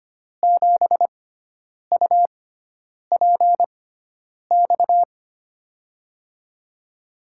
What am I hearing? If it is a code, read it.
Morse code, heard as a keyed tone, about 25 wpm, spaced as for 9 wpm, 700 Hz. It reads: 7UPX
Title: USA Call Signs - 6x Character Spacing - 25wpm